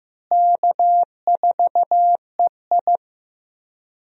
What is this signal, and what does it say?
Morse code, heard as a keyed tone, about 15 wpm, 700 Hz. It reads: K4EI
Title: USA Call Signs - 4 Characters - Repeated 3x - 15wpm